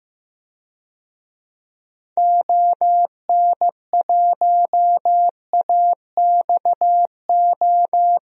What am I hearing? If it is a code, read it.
Morse code, heard as a keyed tone, about 15 wpm, 700 Hz. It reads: ON1AXO
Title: DXCC-100 Call Signs - 6 Characters - 15wpm